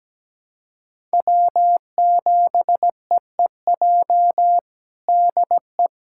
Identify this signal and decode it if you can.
Morse code, heard as a keyed tone, about 17 wpm, 700 Hz. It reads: W7EEJ DE
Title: USA Callsign DE Callsign 17wpm